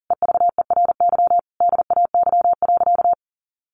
Morse code, heard as a keyed tone, about 37 wpm, 700 Hz. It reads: EVERY DAY.
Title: Sentences from Top 1200 Words 40wpm